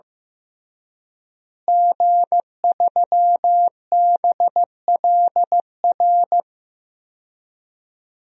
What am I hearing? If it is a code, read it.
Morse code, heard as a keyed tone, about 16 wpm, 700 Hz. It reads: G3BLR
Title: DXCC-100 Call Signs - 5 Characters - Repeated 3x - 15wpm